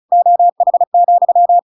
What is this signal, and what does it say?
Morse code, heard as a keyed tone, about 34 wpm, 700 Hz. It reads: OH,
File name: Pinocchio / Pinocchio ch7 35 35 ik1hge